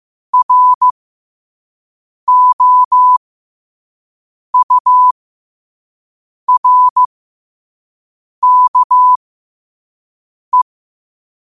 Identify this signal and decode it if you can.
Morse code, heard as a keyed tone, about 15 wpm, spaced as for 5 wpm, 1000 Hz. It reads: ROURKE
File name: 5wpm